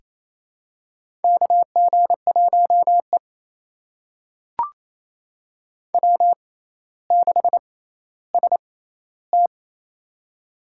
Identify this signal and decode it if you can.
Morse code, heard as a keyed tone, about 28 wpm, spaced as for 14 wpm, 700 Hz. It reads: KG1E W6ST
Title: USA Call Signs - 6x Character Spacing - 28wpm